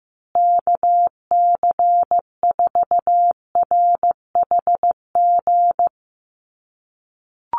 Morse code, heard as a keyed tone, about 15 wpm, 700 Hz. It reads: KC4RHG
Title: USA Call Signs Repeated 2x 15wpm